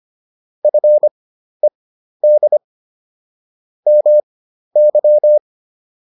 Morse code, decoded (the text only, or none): FED MY